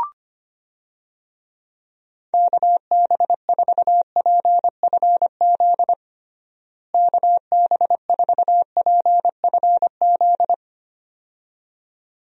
Morse code, heard as a keyed tone, about 25 wpm, 700 Hz. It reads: KB4PFZ KB4PFZ
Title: USA Call Signs - 6 Characters - Repeated 2x - 25wpm